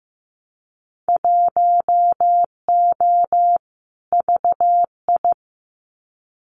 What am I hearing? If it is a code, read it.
Morse code, heard as a keyed tone, about 15 wpm, 700 Hz. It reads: 1O VI
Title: Winter Field Day Exchanges Encore 15wpm